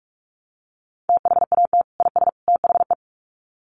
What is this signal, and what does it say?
Morse code, heard as a keyed tone, about 44 wpm, 700 Hz. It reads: THAT IS THE
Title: Sets of 3 Words 45wpm